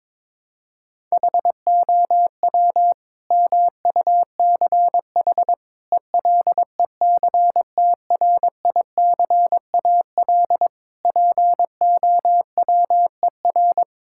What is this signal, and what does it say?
Morse code, heard as a keyed tone, about 22 wpm, 700 Hz. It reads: HOW MUCH ELECTRICAL POWER
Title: Sentences from Top 1000 Words - Review Part 6 - 22wpm